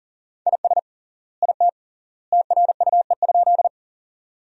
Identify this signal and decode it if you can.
Morse code, heard as a keyed tone, about 37 wpm, 700 Hz. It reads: IS IT TRUE?
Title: 3x Word-Spacing - Sentences from Top 400 Words - 40wpm